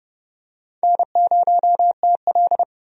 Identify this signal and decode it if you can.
Morse code, heard as a keyed tone, about 32 wpm, 700 Hz. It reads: N0TL
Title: USA Call Signs - 4 Characters - 30wpm